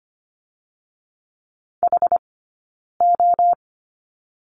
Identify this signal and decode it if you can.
Morse code, heard as a keyed tone, about 25 wpm, spaced as for 9 wpm, 700 Hz. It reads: HO
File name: Lesson 17d - Letter u - Two Characters 25wpm